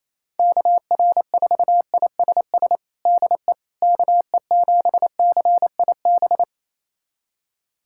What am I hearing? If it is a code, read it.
Morse code, heard as a keyed tone, about 28 wpm, 700 Hz. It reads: KR4ISS DE KE7CIB
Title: USA Callsign DE Callsign - 6 Characters - 28wpm